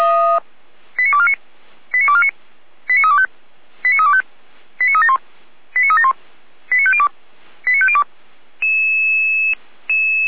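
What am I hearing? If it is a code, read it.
Five-tone selective call (ZVEI), heard as quick runs of five tones, tones of about 70 ms. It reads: 89249 89249 89325 89325 89371 89371 89502 89502
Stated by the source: ZVEI five-tone call received on SDR.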